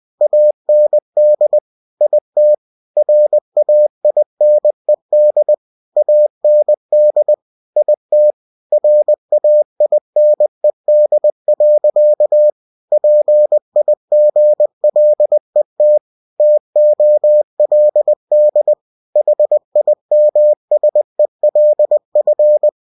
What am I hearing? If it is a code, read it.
Morse code, heard as a keyed tone, about 20 wpm, 600 Hz. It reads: AND IT RAINED AND IT RAINED. PIGLET TOLD HIMSELF
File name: cw-books / winnie-the-pooh / winnie-20wpm-page-0090